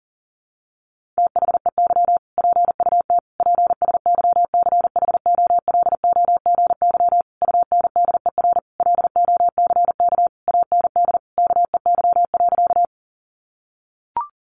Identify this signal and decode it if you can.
Morse code, heard as a keyed tone, about 40 wpm, 700 Hz. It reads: THEY PUT PSYCHOLOGY UNDER LOCK AND KEY.
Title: Sentences from Top 1400 Words 40wpm